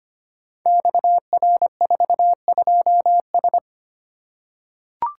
Morse code, heard as a keyed tone, about 25 wpm, 700 Hz. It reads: XR42S
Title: Code Groups - 5 Letter:Numbers - 25wpm